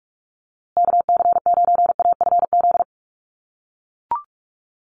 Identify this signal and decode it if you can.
Morse code, heard as a keyed tone, about 44 wpm, 690 Hz. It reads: KC9AFZ
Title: USA Call Signs Repeated 3x 45wpm